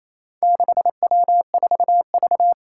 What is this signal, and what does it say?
Morse code, heard as a keyed tone, about 28 wpm, 700 Hz. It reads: 6W4V
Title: Four Challenging Letter-Number Combinations 28wpm